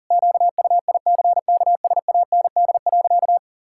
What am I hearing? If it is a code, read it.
Morse code, heard as a keyed tone, about 38 wpm, 700 Hz. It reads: QUICKSAND.
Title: Pangram Challenge v2 - 40wpm